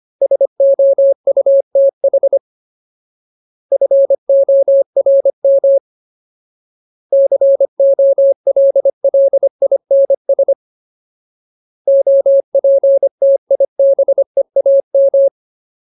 Morse code, heard as a keyed tone, about 25 wpm, 550 Hz. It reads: SOUTH FORM COLLINS OPTIBEAM
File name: Ultimate Word List / 25WPM Ultimate